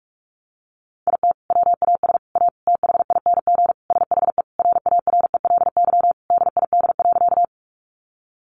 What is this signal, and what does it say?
Morse code, heard as a keyed tone, about 44 wpm, 700 Hz. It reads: IT WAS A THING SHE RARELY DID.
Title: Sentences from Top 1400 Words 45wpm